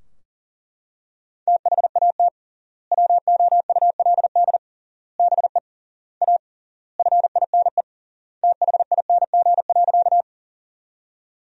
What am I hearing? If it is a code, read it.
Morse code, heard as a keyed tone, about 40 wpm, 700 Hz. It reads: THAT WOULD BE A FINE THING.
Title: 3x Word-Spacing - Sentences from Top 500 Words - 40wpm